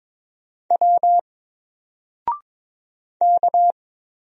Morse code, heard as a keyed tone, about 22 wpm, 700 Hz. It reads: W K
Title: Single Letter-Number Alt 22wpm